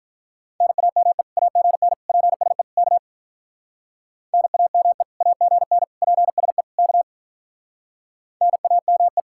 Morse code, heard as a keyed tone, about 44 wpm, 700 Hz. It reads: NAME AGN PSE K NAME AGN PSE K NAME